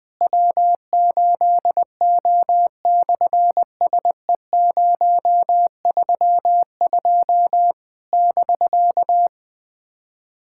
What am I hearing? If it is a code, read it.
Morse code, heard as a keyed tone, about 20 wpm, 700 Hz. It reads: W8O/SE032 <BK>
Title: SOTA QSO Practice 20wpm